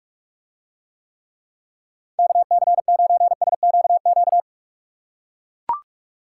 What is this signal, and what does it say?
Morse code, heard as a keyed tone, about 43 wpm, 700 Hz. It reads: KC9SQQ